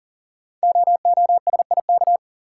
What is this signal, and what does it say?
Morse code, heard as a keyed tone, about 42 wpm, 690 Hz. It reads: OOSIK
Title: Code Groups - 5 Letters - 40wpm